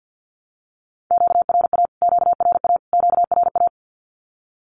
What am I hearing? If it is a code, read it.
Morse code, heard as a keyed tone, about 50 wpm, 700 Hz. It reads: QRA QRA QRA
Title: Q-Codes Repeated 3x 50wpm